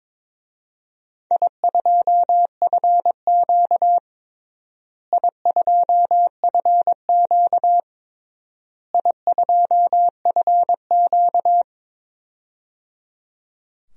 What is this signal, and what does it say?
Morse code, heard as a keyed tone, about 22 wpm, 700 Hz. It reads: I2FQ I2FQ I2FQ
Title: DXCC-100 Call Signs - 4 Characters - Repeated 3x - 22wpm